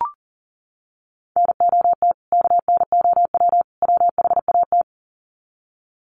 Morse code, heard as a keyed tone, about 40 wpm, 700 Hz. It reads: NOT KNOW WHAT
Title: Sets of 3 Words 40wpm